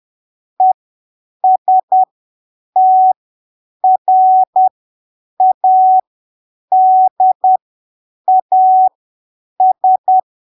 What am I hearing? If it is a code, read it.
Morse code, heard as a keyed tone, about 10 wpm, spaced as for 7 wpm, 750 Hz. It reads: ESTRADAS
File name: arquivos / texto / texto 19 10ppm